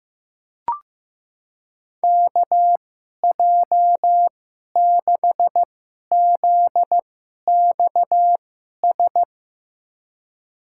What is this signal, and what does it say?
Morse code, heard as a keyed tone, about 15 wpm, spaced as for 11 wpm, 700 Hz. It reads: KJ6ZXS
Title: USA Call Signs - 6 Characters - 2x Character Spacing - 15wpm